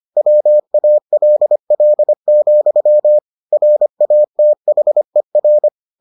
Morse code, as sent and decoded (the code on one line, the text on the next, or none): .-- .- .-.. .-.. --..-- / .-. .- - .... .-.
WALL, RATHER